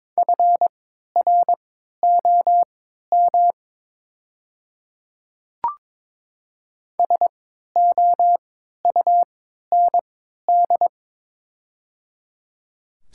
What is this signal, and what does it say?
Morse code, heard as a keyed tone, about 22 wpm, spaced as for 13 wpm, 700 Hz. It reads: FROM SOUND
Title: 3x Character Spacing - Top 100 Words - 22wpm